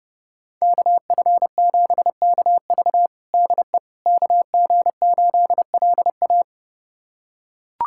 Morse code, heard as a keyed tone, about 30 wpm, 700 Hz. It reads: KF7KV DE KG8LA
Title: USA Callsign DE Callsign - 5 Characters - 30wpm